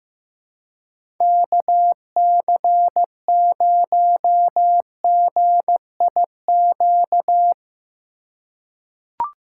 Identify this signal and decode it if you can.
Morse code, heard as a keyed tone, about 15 wpm, 700 Hz. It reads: KC0GIQ
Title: USA Call Signs - 6 Characters - 3x Character Spacing - 15wpm